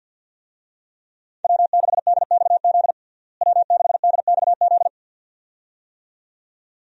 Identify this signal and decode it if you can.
Morse code, heard as a keyed tone, about 47 wpm, 700 Hz. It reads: W6DXZ W6DXZ
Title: USA Call Signs - 5 Characters - Repeated 2x - 50wpm